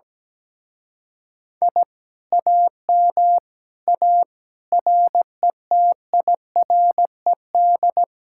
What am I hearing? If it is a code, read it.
Morse code, heard as a keyed tone, about 17 wpm, 700 Hz. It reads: I AM A RETIRED
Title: Speed-Racing Common QSO Phrases - 17wpm